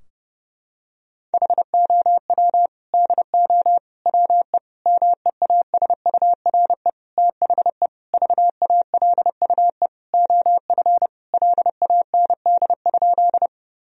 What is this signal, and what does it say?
Morse code, heard as a keyed tone, about 30 wpm, 700 Hz. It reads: HOW DO WE MEASURE THE VALUE OF LAND?
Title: Sentences from Top 1000 Words - Review Part 3 - 30wpm